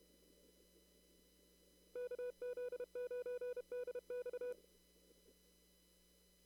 Morse code, heard as a keyed tone, about 32 wpm, 490 Hz. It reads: KZ9DX
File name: CW / kt5j wpx cw-op-k5pi